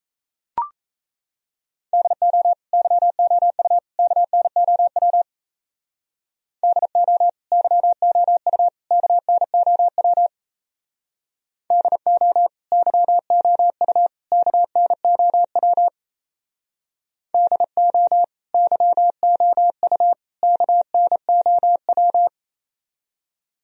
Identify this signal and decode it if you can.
Morse code, heard as a keyed tone, about 33 wpm, 700 Hz. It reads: DO YOU KNOW DO YOU KNOW DO YOU KNOW DO YOU KNOW
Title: Speed-Racing - Sets of 3 Words 28wpm